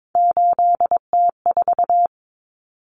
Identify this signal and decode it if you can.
Morse code, heard as a keyed tone, about 22 wpm, 700 Hz. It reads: 8T4
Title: Common Cut Numbers - 3-Digit 22wpm